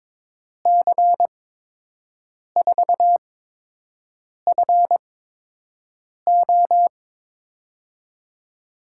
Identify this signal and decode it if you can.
Morse code, heard as a keyed tone, about 22 wpm, spaced as for 6 wpm, 700 Hz. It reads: C4FO
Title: USA Call Signs - 5 Characters - 8x Character Spacing - 22wpm